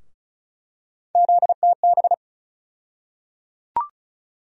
Morse code, heard as a keyed tone, about 34 wpm, 700 Hz. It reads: ZTB
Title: USA Call Sign Postfixes - 3 Letters 35wpm